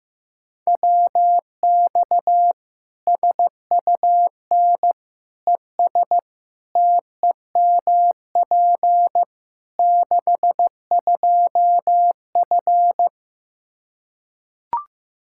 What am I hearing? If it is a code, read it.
Morse code, heard as a keyed tone, about 15 wpm, 700 Hz. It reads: WX SUN ES TEMP 62F
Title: Practice QSO - Exchange 2 - Info - 15wpm